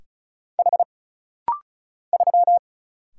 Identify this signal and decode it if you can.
Morse code, heard as a keyed tone, about 36 wpm, 700 Hz. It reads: H 3
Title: Single Letter-Number - Rapid-Fire 35wpm